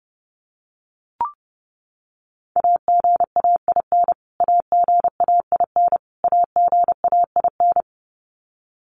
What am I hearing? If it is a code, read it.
Morse code, heard as a keyed tone, about 28 wpm, 700 Hz. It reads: AGAIN AGAIN AGAIN
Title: Top 500 Words Repeated 30wpm